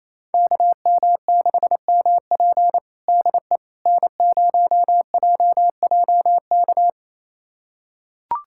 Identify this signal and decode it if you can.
Morse code, heard as a keyed tone, about 28 wpm, 700 Hz. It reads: KM6MP DE N0JJK
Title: USA Callsign DE Callsign - 5 Characters - 28wpm